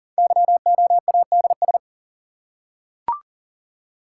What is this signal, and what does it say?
Morse code, heard as a keyed tone, about 38 wpm, 690 Hz. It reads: YOADS
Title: Code Groups - 5 Letter:Numbers - 40wpm